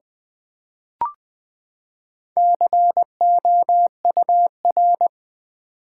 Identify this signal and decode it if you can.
Morse code, heard as a keyed tone, about 20 wpm, 700 Hz. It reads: COUR
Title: Build A Word - 4 Letters At A Time - 8-Letter Words - 20wpm